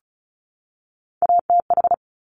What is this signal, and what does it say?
Morse code, heard as a keyed tone, about 36 wpm, 700 Hz. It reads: ATH
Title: Top Three Letter Combinations in Words 35wpm